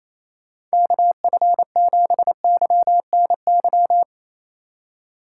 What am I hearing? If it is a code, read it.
Morse code, heard as a keyed tone, about 35 wpm, 700 Hz. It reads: KF7YNY
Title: Speed Racing - USA 2x3 Call Signs 28wpm